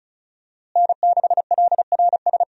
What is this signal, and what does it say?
Morse code, heard as a keyed tone, about 34 wpm, 700 Hz. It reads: N6LRS
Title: USA Call Signs - 5 Characters - Repeated 3x - 35wpm